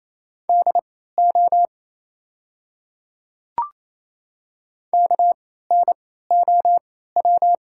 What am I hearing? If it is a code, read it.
Morse code, heard as a keyed tone, about 28 wpm, spaced as for 16 wpm, 690 Hz. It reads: DO KNOW
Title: 3x Character Spacing - Top 100 Words - 28wpm